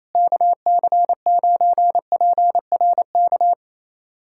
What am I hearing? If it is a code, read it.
Morse code, heard as a keyed tone, about 28 wpm, 700 Hz. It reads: KC9PRK
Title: USA Callsign DE Callsign - 6 Characters - 28wpm